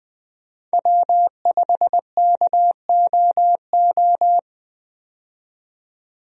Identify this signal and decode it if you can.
Morse code, heard as a keyed tone, about 20 wpm, 700 Hz. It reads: W5KOO
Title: USA Call Signs - 5 Characters - 20wpm